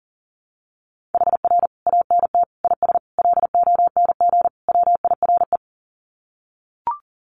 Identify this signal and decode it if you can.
Morse code, heard as a keyed tone, about 40 wpm, 700 Hz. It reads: HR ANT IS LONG WIRE
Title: Common QSO Phrases 40wpm